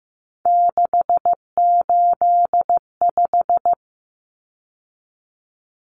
Morse code, heard as a keyed tone, about 15 wpm, 700 Hz. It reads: 685
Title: Numbers - 3-Digit 15wpm